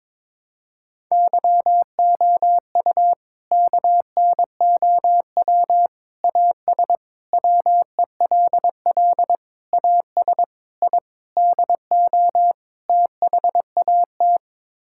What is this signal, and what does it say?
Morse code, heard as a keyed tone, about 22 wpm, 700 Hz. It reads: YOU KNOW AS WELL AS I DO THAT
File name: practice / 22wpm / Sets of 8 Words - 22wpm